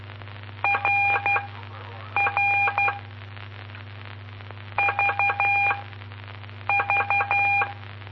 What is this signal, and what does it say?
Morse code, heard as a keyed tone, about 12 wpm, spaced as for 7 wpm, 2400 Hz, with noise. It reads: RR VV
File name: pp5bar / aulas de telegrafia / Aula-06